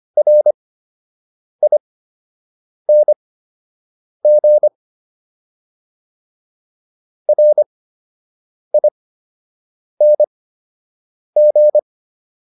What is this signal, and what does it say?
Morse code, heard as a keyed tone, about 25 wpm, spaced as for 7 wpm, 600 Hz. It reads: RING RING